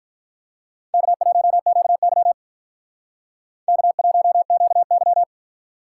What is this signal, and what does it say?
Morse code, heard as a keyed tone, about 45 wpm, 700 Hz. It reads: K1QY K1QY